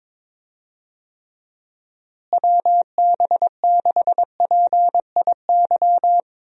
Morse code, heard as a keyed tone, about 22 wpm, 700 Hz. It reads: WB6PIY
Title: USA Call Signs - 6 Characters - Repeated 2x - 22wpm